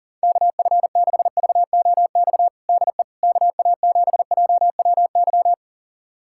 Morse code, heard as a keyed tone, about 37 wpm, 700 Hz. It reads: KF6VOX DE KA7JWY
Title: USA Callsign DE Callsign - 6 Characters - 40wpm